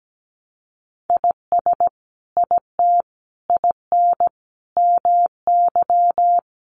Morse code, heard as a keyed tone, about 17 wpm, 700 Hz. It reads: IS IT IN MY
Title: Sentences from Top 1200 Words 17wpm